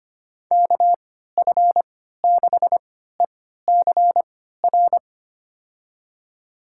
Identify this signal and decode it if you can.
Morse code, heard as a keyed tone, about 25 wpm, spaced as for 14 wpm, 700 Hz. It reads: KF6ECR
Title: USA Call Signs - 3x Character Spacing - 25wpm